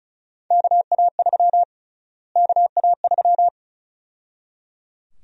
Morse code, heard as a keyed tone, about 34 wpm, 700 Hz. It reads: KA3 KA3
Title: USA Call Sign Prefixes - Repeated 2x - 35wpm